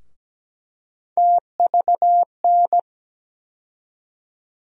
Morse code, heard as a keyed tone, about 17 wpm, 700 Hz. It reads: TVN
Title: USA Call Sign Postfixes - 3 Letters 17wpm